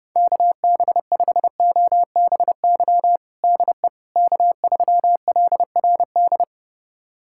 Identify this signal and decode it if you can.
Morse code, heard as a keyed tone, about 30 wpm, 700 Hz. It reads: KB5OBY DE K3LRD
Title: USA Callsign DE Callsign 30wpm